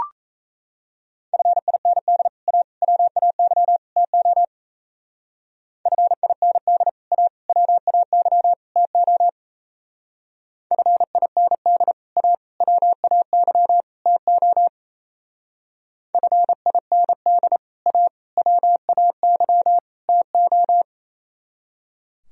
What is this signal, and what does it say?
Morse code, heard as a keyed tone, about 34 wpm, 700 Hz. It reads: FIND A WAY TO FIND A WAY TO FIND A WAY TO FIND A WAY TO
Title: Speed-Racing - Sets of 4 Words 28wpm